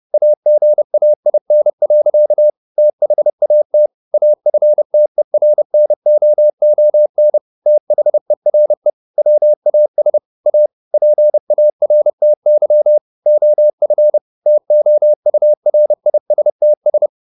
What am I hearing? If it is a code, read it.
Morse code, heard as a keyed tone, about 30 wpm, 600 Hz. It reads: AGAIN. THAT AFTERNOON THERE WAS A PARTY OF TOURISTS